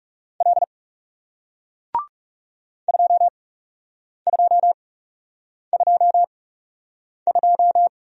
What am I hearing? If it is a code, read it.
Morse code, heard as a keyed tone, about 42 wpm, spaced as for 8 wpm, 700 Hz. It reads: L 2222